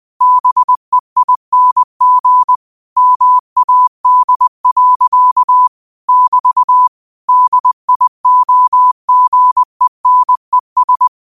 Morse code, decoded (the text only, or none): BEING MAD. <BT> DIOGENES